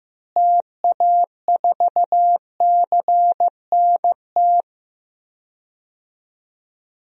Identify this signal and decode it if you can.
Morse code, heard as a keyed tone, about 15 wpm, 700 Hz. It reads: TA4CNT
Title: DXCC-100 Call Signs - 6 Characters - Repeated 2x - 15wpm